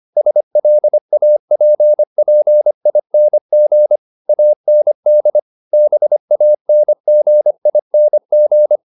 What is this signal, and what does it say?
Morse code, heard as a keyed tone, about 25 wpm, 600 Hz. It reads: SLAPPING AND BANGING
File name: the old man and the sea part 1 25wpm